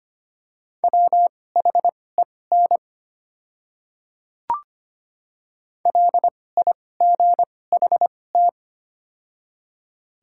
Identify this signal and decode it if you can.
Morse code, heard as a keyed tone, about 25 wpm, spaced as for 18 wpm, 700 Hz. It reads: WHEN LIGHT
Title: 2x Character Spacing - Top 300 Words - 25wpm